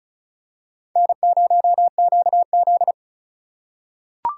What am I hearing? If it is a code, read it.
Morse code, heard as a keyed tone, about 34 wpm, 700 Hz. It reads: N0QZ
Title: USA Call Signs - 4 Characters - 4x Character Spacing - 35wpm